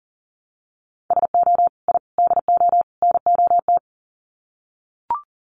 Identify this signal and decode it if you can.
Morse code, heard as a keyed tone, about 40 wpm, 700 Hz. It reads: SO I DO NOT
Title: Sets of 4 Words 40wpm